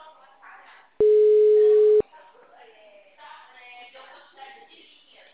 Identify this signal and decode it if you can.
Ringback tone (Europe, 425 Hz), with speech.
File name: out- -2015- - - .28536